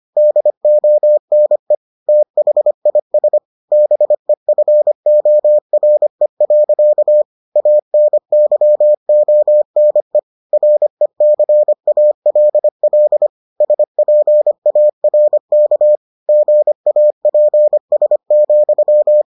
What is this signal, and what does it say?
Morse code, heard as a keyed tone, about 25 wpm, 600 Hz. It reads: DONE THIS BEFORE. ANYONE RECALL SPARK GAPS,